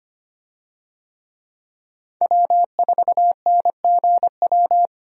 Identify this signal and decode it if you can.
Morse code, heard as a keyed tone, about 25 wpm, 700 Hz. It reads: W4NGW
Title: USA Call Signs - 5 Characters - 25wpm